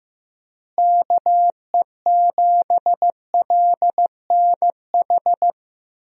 Morse code, heard as a keyed tone, about 15 wpm, 700 Hz. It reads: KE7LNH
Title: USA Call Signs - 6 Characters - Repeated 3x - 15wpm